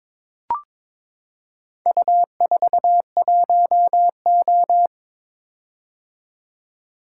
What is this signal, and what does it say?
Morse code, heard as a keyed tone, about 23 wpm, 700 Hz. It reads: U41O
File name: practice / 22wpm / Four Letter-Number Combinations - 22wpm